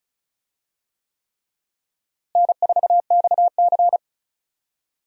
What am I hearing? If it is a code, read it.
Morse code, heard as a keyed tone, about 36 wpm, 700 Hz. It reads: N4XC
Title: USA Call Signs - 4 Characters - Repeated 3x - 35wpm